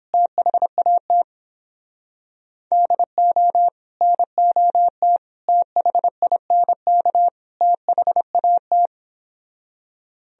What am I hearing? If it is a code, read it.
Morse code, heard as a keyed tone, about 31 wpm, 700 Hz. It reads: THAT DO NOT THINK THAT